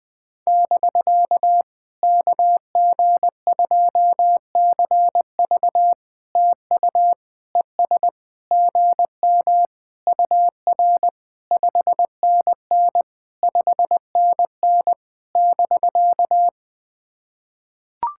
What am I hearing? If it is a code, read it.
Morse code, heard as a keyed tone, about 18 wpm, 700 Hz. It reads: <BK> KG2CV TU ES GM UR 5NN 5NN <BK>
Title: POTA QSO Practice - 20wpm